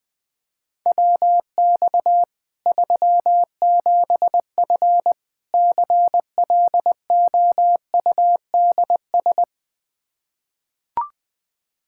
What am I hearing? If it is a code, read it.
Morse code, heard as a keyed tone, about 20 wpm, 700 Hz. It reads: WX 37F CLOUDS
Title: Practice QSO - Exchange 2 - Info - 20wpm